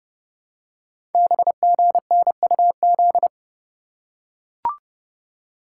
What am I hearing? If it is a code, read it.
Morse code, heard as a keyed tone, about 30 wpm, 700 Hz. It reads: BGNUZ
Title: Code Groups - 5 Letters - 30wpm